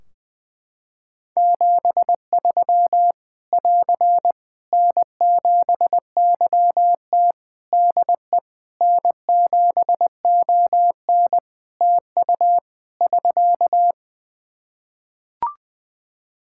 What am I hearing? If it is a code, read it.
Morse code, heard as a keyed tone, about 21 wpm, 700 Hz. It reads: 73 <AR> N7YT DE N7ON TU <SK>
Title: Practice QSO - Exchange 4 - Ending - 20wpm